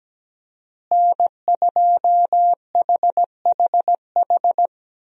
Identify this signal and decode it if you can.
Morse code, heard as a keyed tone, about 17 wpm, 700 Hz. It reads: N2HHH
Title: USA Call Signs - 5 Characters - Repeated 2x - 17wpm